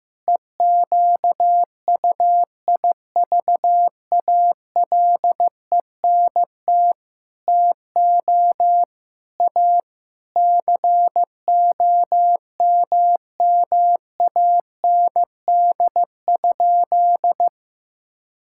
Morse code, heard as a keyed tone, about 15 wpm, 700 Hz. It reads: EQUIVALENT TO A COMMAND?
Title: Sentences from Top 1600 Words 15wpm